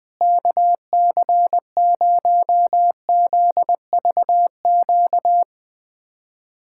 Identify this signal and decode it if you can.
Morse code, heard as a keyed tone, about 20 wpm, 700 Hz. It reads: KC0ZVQ
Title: USA Callsign DE Callsign 20wpm